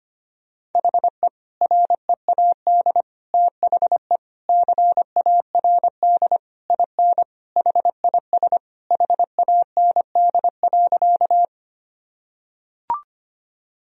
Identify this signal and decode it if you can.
Morse code, heard as a keyed tone, about 25 wpm, 700 Hz. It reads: HE READ THE CARD IN HIS HAND.
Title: Sentences from Top 1000 Words - Review Part 4 - 25wpm